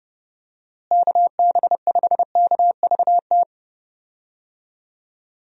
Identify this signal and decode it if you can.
Morse code, heard as a keyed tone, about 31 wpm, 700 Hz. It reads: KB5KVT
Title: USA Call Signs - 6 Characters - Repeated 3x - 30wpm